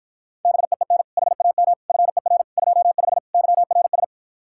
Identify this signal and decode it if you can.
Morse code, heard as a keyed tone, about 52 wpm, 700 Hz. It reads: BEEN HAM FER 25 YRS